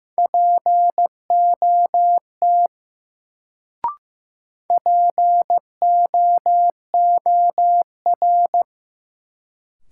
Morse code, heard as a keyed tone, about 15 wpm, 700 Hz. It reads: POT POOR
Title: Sentences from Top 1000 Words - New Word Drill - 15wpm